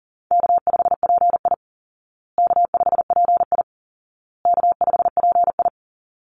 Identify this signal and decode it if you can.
Morse code, heard as a keyed tone, about 40 wpm, 700 Hz. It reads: K5PI K5PI K5PI
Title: Easy Call Signs Repeated 3x - 40wpm